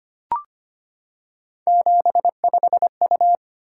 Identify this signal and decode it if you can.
Morse code, heard as a keyed tone, about 25 wpm, 700 Hz. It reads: 75U
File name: Three Letter-Number Combinations - 25wpm